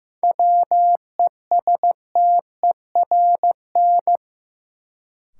Morse code, heard as a keyed tone, about 15 wpm, 700 Hz. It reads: WESTERN
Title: Sentences from Top 1000 Words - Review Part 7 - New Word Drill - 15wpm